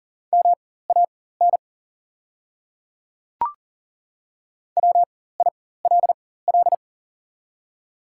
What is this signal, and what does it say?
Morse code, heard as a keyed tone, about 40 wpm, spaced as for 19 wpm, 700 Hz. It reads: MAN WILL
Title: 4x Character Spacing - Top 300 Words - 40wpm